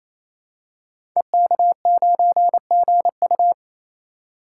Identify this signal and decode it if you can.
Morse code, heard as a keyed tone, about 27 wpm, 700 Hz. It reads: EK9GU
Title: Code Groups - 5 Letter:Numbers - 28wpm